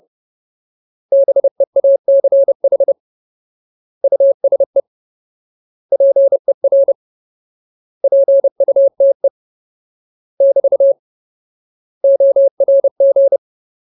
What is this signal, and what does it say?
Morse code, heard as a keyed tone, about 30 wpm, 550 Hz. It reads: BEACH USE PER PUTE <BT> ORG
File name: Ultimate Word List / 30WPM Ultimate